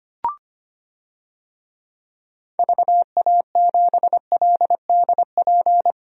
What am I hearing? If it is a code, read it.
Morse code, heard as a keyed tone, about 25 wpm, 700 Hz. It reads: VA7LDP
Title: DXCC-100 Call Signs - 6 Characters - 25wpm